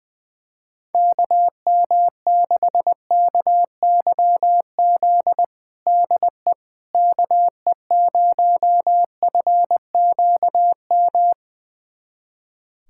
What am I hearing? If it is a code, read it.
Morse code, heard as a keyed tone, about 21 wpm, 700 Hz. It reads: KM6KYZ DE KE0FQM
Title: USA Callsign DE Callsign 20wpm